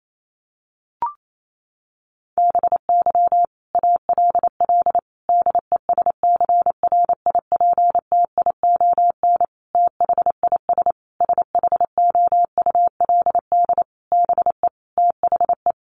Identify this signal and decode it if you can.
Morse code, heard as a keyed tone, about 28 wpm, 700 Hz. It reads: BY ALL DESCRIPTION THIS SHOULD BE THE
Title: Sentences from Top 1300 Words 28wpm